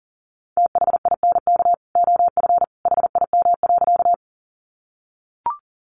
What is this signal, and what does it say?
Morse code, heard as a keyed tone, about 40 wpm, 700 Hz. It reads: THINK OF HIM.
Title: Sentences from Top 200 Words 40wpm